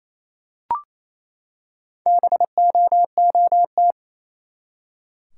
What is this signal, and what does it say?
Morse code, heard as a keyed tone, about 28 wpm, 700 Hz. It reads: BOOT
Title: Four Letter Words - 28wpm